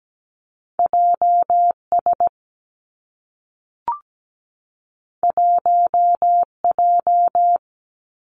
Morse code, heard as a keyed tone, about 17 wpm, 700 Hz. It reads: JS 1J
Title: Two Challenging Letter-Number Combinations 17wpm